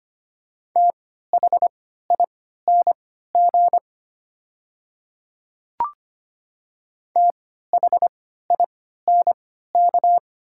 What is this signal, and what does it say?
Morse code, heard as a keyed tone, about 25 wpm, spaced as for 14 wpm, 700 Hz. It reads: THING THINK
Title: 3x Character Spacing - Top 500 Words - 25wpm